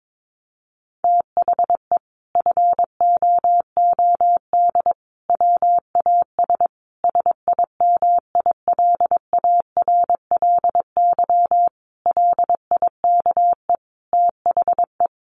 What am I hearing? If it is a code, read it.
Morse code, heard as a keyed tone, about 22 wpm, 700 Hz. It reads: THE FOOD WAS SIMILARLY LIKE THE
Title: Sentences from Top 1400 Words 22wpm